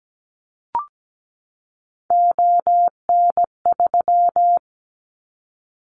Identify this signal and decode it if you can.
Morse code, heard as a keyed tone, about 17 wpm, 700 Hz. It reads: ON3
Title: Callsign Prefixes for Top 100 DXCC - 17wpm